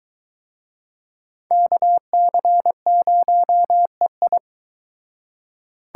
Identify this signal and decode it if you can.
Morse code, heard as a keyed tone, about 23 wpm, 700 Hz. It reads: KC0EI